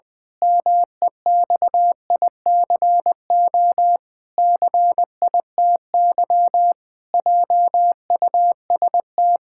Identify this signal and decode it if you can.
Morse code, heard as a keyed tone, about 20 wpm, 700 Hz. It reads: MEXICO CITY JUST
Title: Pangram Challenge v2 - 20wpm